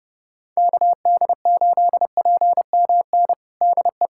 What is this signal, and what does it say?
Morse code, heard as a keyed tone, about 30 wpm, 700 Hz. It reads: KD8PMN DE
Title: USA Callsign DE Callsign - 6 Characters - 30wpm